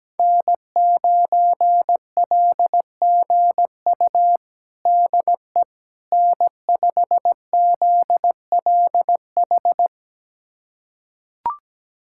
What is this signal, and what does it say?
Morse code, heard as a keyed tone, about 17 wpm, 700 Hz. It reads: N9LGU DE N5ZLH
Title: USA Callsign DE Callsign - 5 Characters - 17wpm